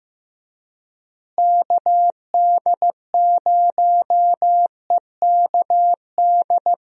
Morse code, heard as a keyed tone, about 16 wpm, 700 Hz. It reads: KD0EKD
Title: USA Call Signs - 6 Characters - Repeated 3x - 15wpm